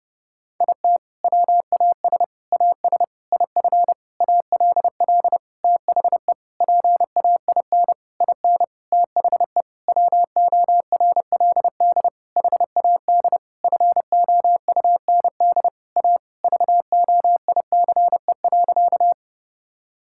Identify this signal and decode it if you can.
Morse code, heard as a keyed tone, about 29 wpm, 700 Hz. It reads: IT WAS AS IF ALL THE PAIN IN THE WORLD HAD FOUND A VOICE.
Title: Sentences from Top 1600 Words 30wpm